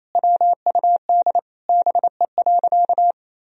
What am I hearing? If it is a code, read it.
Morse code, heard as a keyed tone, about 28 wpm, 700 Hz. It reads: WUD BE.
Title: Sentences from Top 200 Words and Abbreviations 28wpm